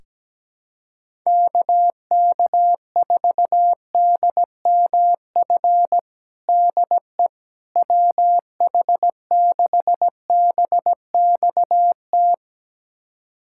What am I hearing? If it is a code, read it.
Morse code, heard as a keyed tone, about 17 wpm, 700 Hz. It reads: KK4DMF DE WH6BXT
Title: USA Callsign DE Callsign 17wpm